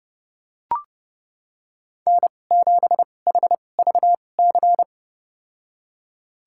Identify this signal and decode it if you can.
Morse code, heard as a keyed tone, about 30 wpm, spaced as for 22 wpm, 700 Hz. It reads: N7HVC
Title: USA Call Signs - 2x Character Spacing - 30wpm